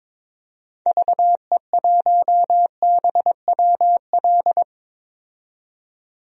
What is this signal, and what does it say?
Morse code, heard as a keyed tone, about 22 wpm, 700 Hz. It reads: VE1BWL
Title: DXCC-100 Call Signs - 6 Characters - 22wpm